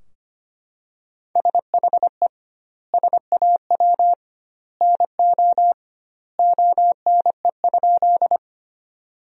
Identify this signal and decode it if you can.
Morse code, heard as a keyed tone, about 25 wpm, 700 Hz. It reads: SHE SAW NO ONE?
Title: 2x Word-Spacing - Sentences from Top 300 Words - 25wpm